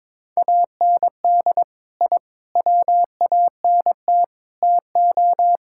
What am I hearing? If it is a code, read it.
Morse code, heard as a keyed tone, about 22 wpm, 700 Hz. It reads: AND I WANT TO
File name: practice / 22wpm / Sentences with 4-Word Phrases - 22wpm